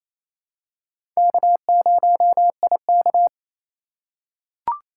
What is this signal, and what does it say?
Morse code, heard as a keyed tone, about 28 wpm, 700 Hz. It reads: K0IK
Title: USA Call Signs - 4 Characters - 2x Character Spacing - 28wpm